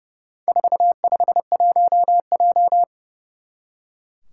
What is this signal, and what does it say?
Morse code, heard as a keyed tone, about 30 wpm, 700 Hz. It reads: 451J
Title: Four Challenging Letter-Number Combinations 30wpm